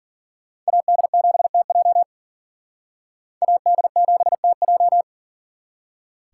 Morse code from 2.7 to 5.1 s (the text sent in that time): AD7TJ